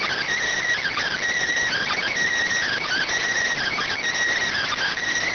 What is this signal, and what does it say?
Emergency Alert System (SAME) tones, heard as harsh buzzing bursts. It is heard from 0.8 to 5.0 s.